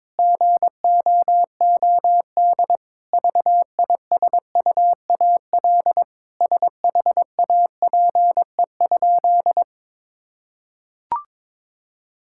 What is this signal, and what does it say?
Morse code, heard as a keyed tone, about 21 wpm, 700 Hz. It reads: GOOD VISUAL SHAPE?
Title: Sentences from Top 1000 Words - Review Part 6 - 22wpm